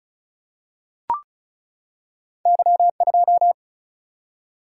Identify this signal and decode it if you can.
Morse code, heard as a keyed tone, about 36 wpm, 700 Hz. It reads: Y2
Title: Single-Letter Single-Number 35wpm